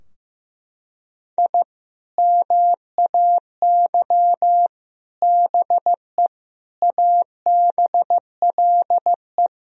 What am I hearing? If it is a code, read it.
Morse code, heard as a keyed tone, about 15 wpm, 700 Hz. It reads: I MAY BE ABLE
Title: Sentences from Top 1000 Words - Review Part 2 - 15wpm